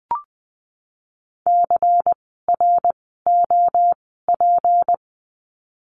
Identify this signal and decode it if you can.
Morse code, heard as a keyed tone, about 20 wpm, spaced as for 15 wpm, 700 Hz. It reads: CROP
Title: 153 Lesson 25f - Letter p - Words 20wpm